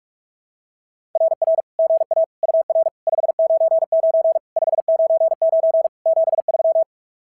Morse code, heard as a keyed tone, about 44 wpm, 660 Hz. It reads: RR GA UR 599 599 73
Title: SOTA QSO Practice 45wpm